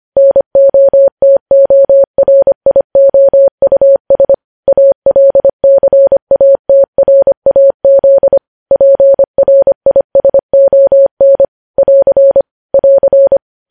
Morse code, heard as a keyed tone, about 25 wpm, 550 Hz. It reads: NOTORIOUS ALCATRAZ PRISON <AR> <AR>
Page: news-headlines-in-morse-code-at-25-wpm